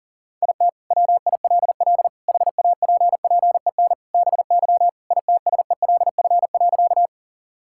Morse code, heard as a keyed tone, about 37 wpm, 700 Hz. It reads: IT WILL HAPPEN BY ITSELF.
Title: Sentences from Top 1000 Words - Review Part 5 - 40wpm